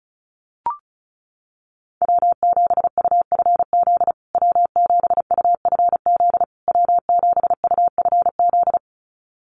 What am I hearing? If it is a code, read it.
Morse code, heard as a keyed tone, about 34 wpm, 700 Hz. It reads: W7UFZ W7UFZ W7UFZ
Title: USA Call Signs Repeated 3x 35wpm